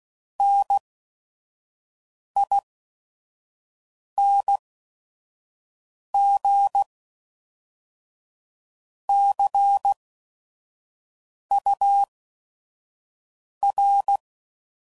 Morse code, heard as a keyed tone, about 16 wpm, spaced as for 5 wpm, 800 Hz. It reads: NING CUR
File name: CW Code Groups @5WPM / Technical Subjects / XC